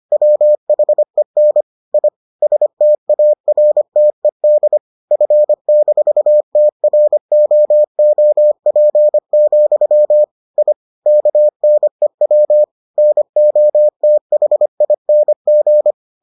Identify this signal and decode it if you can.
Morse code, heard as a keyed tone, about 25 wpm, 600 Hz. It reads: WHEN I STARTED F-TROOP, I KNEW NOTHING